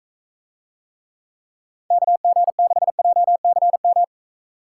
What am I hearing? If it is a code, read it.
Morse code, heard as a keyed tone, about 42 wpm, 700 Hz. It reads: KG6JCM